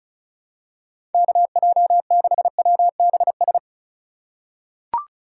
Morse code, heard as a keyed tone, about 34 wpm, 700 Hz. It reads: KJ6WBS
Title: USA Call Signs - 6 Characters - 4x Character Spacing - 35wpm